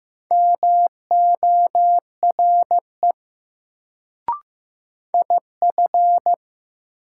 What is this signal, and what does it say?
Morse code, heard as a keyed tone, about 15 wpm, 700 Hz. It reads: MORE IF
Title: Sentences from Top 100 Words - New Word Drill - 15wpm